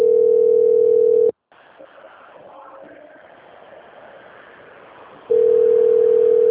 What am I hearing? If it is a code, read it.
Ringback tone (North America), 440 + 480 Hz.
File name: out- - - - - .73013